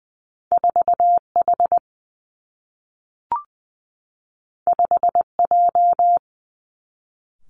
Morse code, heard as a keyed tone, about 20 wpm, 700 Hz. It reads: 4 H 5 J
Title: Two Challenging Letter-Number Combinations 20wpm